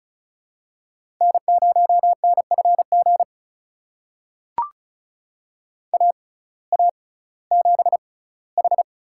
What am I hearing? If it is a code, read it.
Morse code, heard as a keyed tone, about 34 wpm, spaced as for 19 wpm, 700 Hz. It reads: N0NFG AA7H
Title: USA Call Signs - 5 Characters - 6x Character Spacing - 35wpm